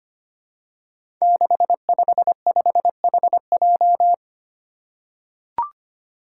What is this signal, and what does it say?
Morse code, heard as a keyed tone, about 25 wpm, 700 Hz. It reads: 655HJ
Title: Five Challenging Letter-Number Combinations 25wpm